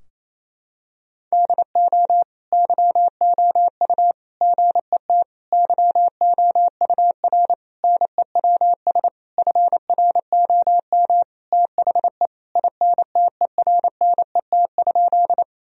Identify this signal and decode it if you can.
Morse code, heard as a keyed tone, about 28 wpm, 700 Hz. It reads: DO YOU GET YOUR NEWS FROM THE INTERNET?
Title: Sentences from Top 1000 Words - Review Part 4 - 28wpm